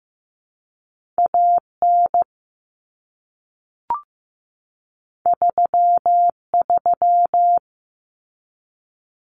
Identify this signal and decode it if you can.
Morse code, heard as a keyed tone, about 15 wpm, 700 Hz. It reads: A N 3 3
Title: Common Cut Numbers - 2-Digit 15wpm